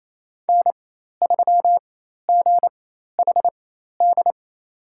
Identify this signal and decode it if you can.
Morse code, heard as a keyed tone, about 28 wpm, spaced as for 13 wpm, 700 Hz. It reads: N3GHD
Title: USA Call Signs - 5 Characters - 4x Character Spacing - 28wpm